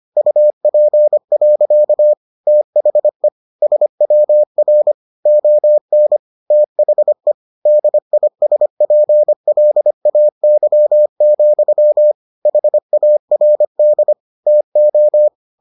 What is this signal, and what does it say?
Morse code, heard as a keyed tone, about 25 wpm, 600 Hz. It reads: UP. THE SWR ON THE DISPLAY, HARD TO